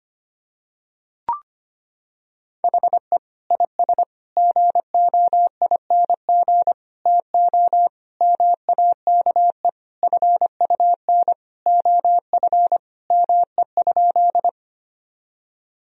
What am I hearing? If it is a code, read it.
Morse code, heard as a keyed tone, about 25 wpm, 700 Hz. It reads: HE IS GOING TO MAKE FUN OF ME?
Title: Sentences from Top 1000 Words - Review Part 4 - 25wpm